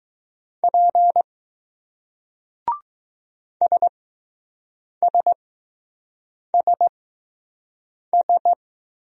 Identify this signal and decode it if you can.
Morse code, heard as a keyed tone, about 20 wpm, spaced as for 6 wpm, 700 Hz. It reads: P SSSS